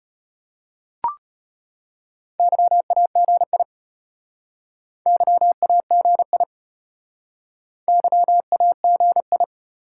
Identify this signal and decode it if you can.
Morse code, heard as a keyed tone, about 33 wpm, 700 Hz. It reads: YAGI YAGI YAGI